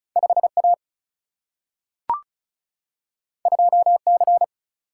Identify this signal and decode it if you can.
Morse code, heard as a keyed tone, about 36 wpm, 700 Hz. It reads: 5 A 2 C
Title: Single-Letter Single-Number 35wpm